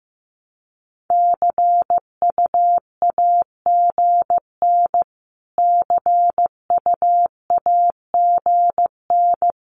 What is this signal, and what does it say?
Morse code, heard as a keyed tone, about 15 wpm, 700 Hz. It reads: CUAGN CUAGN
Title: Abbreviations Repeated 3x - 15wpm